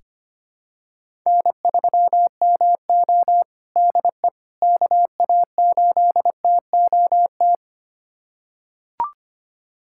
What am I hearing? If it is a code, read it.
Morse code, heard as a keyed tone, about 25 wpm, 700 Hz. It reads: N3MO DE KA8TOT
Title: USA Callsign DE Callsign 25wpm